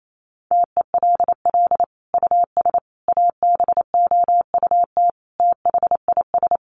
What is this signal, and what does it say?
Morse code, heard as a keyed tone, about 28 wpm, 700 Hz. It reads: TELL US ABOUT THIS
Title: Sentences from Top 1200 Words 28wpm